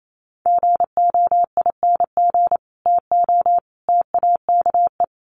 Morse code, heard as a keyed tone, about 28 wpm, 700 Hz. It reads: GOING TO TAKE
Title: Sets of 3 Words 28wpm